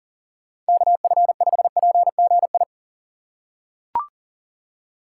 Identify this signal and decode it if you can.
Morse code, heard as a keyed tone, about 39 wpm, 700 Hz. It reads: KF5PGI
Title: USA Call Signs - 6 Characters - 2x Character Spacing - 40wpm